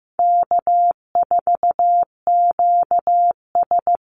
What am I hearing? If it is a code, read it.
Morse code, heard as a keyed tone, about 15 wpm, 700 Hz. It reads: K4QS
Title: NA SOTA Chaser Callsigns 15wpm